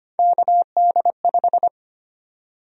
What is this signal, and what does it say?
Morse code, heard as a keyed tone, about 25 wpm, 700 Hz. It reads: KD5
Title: USA Call Sign Prefixes - 25wpm